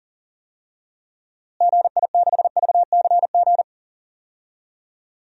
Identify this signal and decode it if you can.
Morse code, heard as a keyed tone, about 40 wpm, 700 Hz. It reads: GI6VCG
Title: DXCC-100 Call Signs - 6 Characters - Repeated 2x - 40wpm